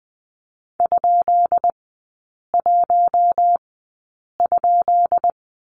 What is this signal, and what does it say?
Morse code, heard as a keyed tone, about 20 wpm, spaced as for 8 wpm, 700 Hz. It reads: ?1?
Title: 133 Lesson 22e - Question Mark - Three Characters 20wpm